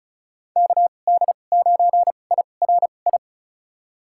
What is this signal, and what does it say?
Morse code, heard as a keyed tone, about 36 wpm, spaced as for 25 wpm, 700 Hz. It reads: KD9IRI
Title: USA Call Signs - 6 Characters - 2x Character Spacing - 35wpm